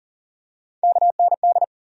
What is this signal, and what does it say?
Morse code, heard as a keyed tone, about 39 wpm, 700 Hz. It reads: KND
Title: USA Call Sign Postfixes - 3 Letters 40wpm